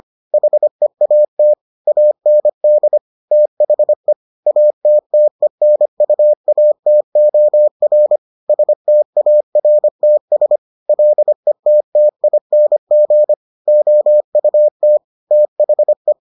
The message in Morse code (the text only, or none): HEAT AND THE ATTENUATOR STARTS LETTING OUT THE